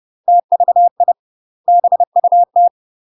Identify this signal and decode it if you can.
Morse code, heard as a keyed tone, about 30 wpm, 700 Hz. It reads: TVI BUT
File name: UK MASTER QSOs TRAINING / 30WPM / UK MASTER QSOs 0023